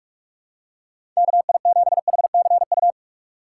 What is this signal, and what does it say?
Morse code, heard as a keyed tone, about 43 wpm, 700 Hz. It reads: KI7HCU
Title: USA Call Signs - 6 Characters - 4x Character Spacing - 45wpm